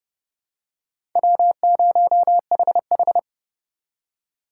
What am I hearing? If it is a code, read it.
Morse code, heard as a keyed tone, about 30 wpm, 700 Hz. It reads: W0HH
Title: USA Call Signs - 4 Characters - 2x Character Spacing - 30wpm